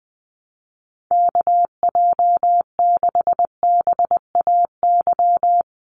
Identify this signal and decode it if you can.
Morse code, heard as a keyed tone, about 20 wpm, 700 Hz. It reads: KJ6BAY
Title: USA Call Signs 20wpm